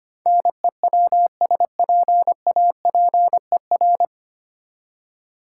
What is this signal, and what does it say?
Morse code, heard as a keyed tone, about 25 wpm, 700 Hz. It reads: NEWSPAPER
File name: Nine Letter Words - 25wpm